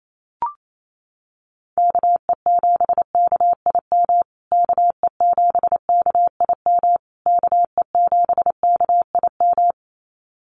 Morse code, heard as a keyed tone, about 28 wpm, 700 Hz. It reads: KE7KIM KE7KIM KE7KIM
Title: USA Call Signs Repeated 3x 28wpm